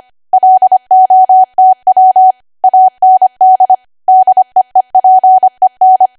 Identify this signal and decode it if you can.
Morse code, heard as a keyed tone, about 25 wpm, 750 Hz. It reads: LOTW AND DEEPEN